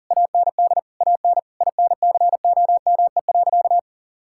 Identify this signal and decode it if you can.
Morse code, heard as a keyed tone, about 40 wpm, 700 Hz. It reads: AND AN INCOME.
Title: Sentences from Top 1000 Words - Review Part 5 - 40wpm